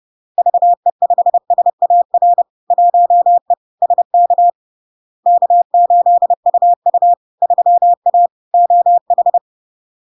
Morse code, heard as a keyed tone, about 30 wpm, 700 Hz. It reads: VE5SAR 1E SK K8UU 3A OH
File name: LARK-FD-2022-Log-in-CW-30wpm-700-Hz